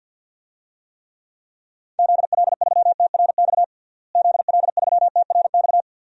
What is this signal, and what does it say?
Morse code, heard as a keyed tone, about 50 wpm, 700 Hz. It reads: ZL3TRX ZL3TRX
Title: DXCC-100 Call Signs - 6 Characters - Repeated 3x - 50wpm